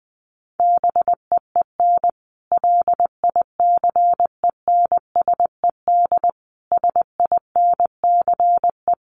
Morse code, heard as a keyed tone, about 19 wpm, 700 Hz. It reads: BEEN LICENSED SINCE
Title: Common QSO Phrases 20wpm